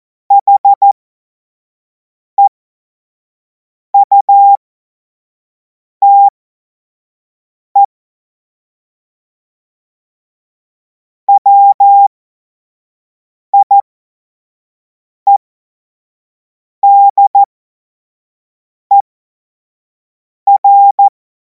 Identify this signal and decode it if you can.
Morse code, heard as a keyed tone, about 14 wpm, spaced as for 5 wpm, 800 Hz. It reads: HEUTE WIEDER
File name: Morsekurs / Witze / Witz 17. Februar 2026 25 F70